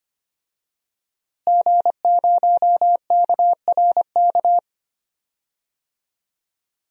Morse code, heard as a keyed tone, about 25 wpm, 700 Hz. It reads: G0KRK
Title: DXCC-100 Call Signs - 5 Characters - Repeated 3x - 25wpm